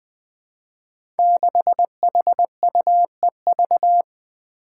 Morse code, heard as a keyed tone, about 20 wpm, 700 Hz. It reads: 6HUEV
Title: Five Challenging Letter-Number Combinations 20wpm